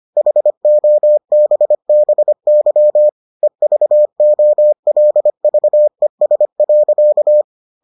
Morse code, heard as a keyed tone, about 25 wpm, 600 Hz. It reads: HOBBY EVOLVES.